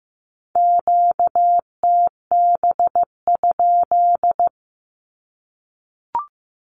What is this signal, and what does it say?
Morse code, heard as a keyed tone, about 15 wpm, 700 Hz. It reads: QTB?
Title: Q-Codes Repeated 3x 15wpm